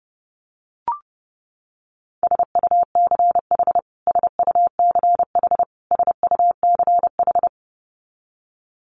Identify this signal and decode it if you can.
Morse code, heard as a keyed tone, about 30 wpm, 700 Hz. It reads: SUCH SUCH SUCH
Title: Top 300 Words Repeated 30wpm